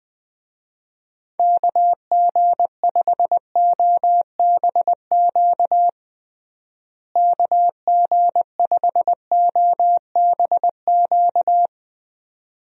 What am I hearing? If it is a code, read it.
Morse code, heard as a keyed tone, about 20 wpm, 700 Hz. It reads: KG5OBQ KG5OBQ
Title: USA Call Signs - 6 Characters - Repeated 2x - 20wpm